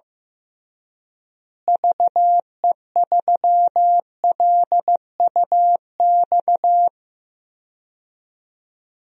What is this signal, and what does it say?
Morse code, heard as a keyed tone, about 15 wpm, 700 Hz. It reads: VE3LUX
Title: DXCC-100 Call Signs - 6 Characters - Repeated 3x - 15wpm